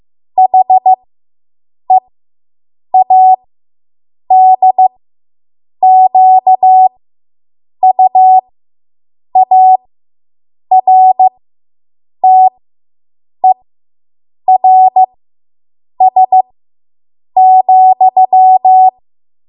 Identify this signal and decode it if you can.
Morse code, heard as a keyed tone, about 15 wpm, spaced as for 7 wpm, 750 Hz. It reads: HEADQUARTERS,